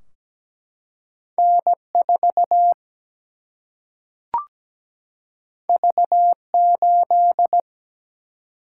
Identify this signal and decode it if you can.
Morse code, heard as a keyed tone, about 17 wpm, 700 Hz. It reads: N4 V8
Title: Single-Letter Single-Number 17wpm